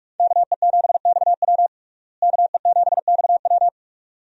Morse code, heard as a keyed tone, about 46 wpm, 700 Hz. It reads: KE7XW KE7XW
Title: USA Call Signs - 5 Characters - Repeated 2x - 45wpm